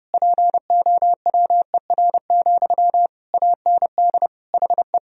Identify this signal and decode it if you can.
Morse code, heard as a keyed tone, about 30 wpm, 700 Hz. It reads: POWER, AND HE
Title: Sentences from Top 1000 Words - Review Part 2 - 30wpm